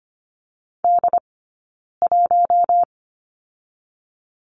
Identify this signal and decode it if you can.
Morse code, heard as a keyed tone, about 25 wpm, 700 Hz. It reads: D 1
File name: course / 25:10wpm / 099 Lesson 17d - Letter u - Two Characters 25wpm